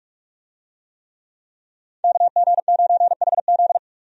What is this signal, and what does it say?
Morse code, heard as a keyed tone, about 41 wpm, 700 Hz. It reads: KG9HZ
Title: USA Call Signs - 5 Characters - 45wpm